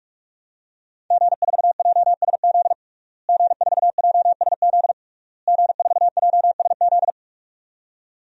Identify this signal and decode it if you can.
Morse code, heard as a keyed tone, about 46 wpm, 700 Hz. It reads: G4JSZ G4JSZ G4JSZ
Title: DXCC-100 Call Signs - 5 Characters - Repeated 3x - 45wpm